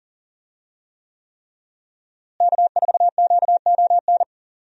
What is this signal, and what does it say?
Morse code, heard as a keyed tone, about 40 wpm, 690 Hz. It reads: K4QON
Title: USA Call Signs - 5 Characters - Repeated 3x - 40wpm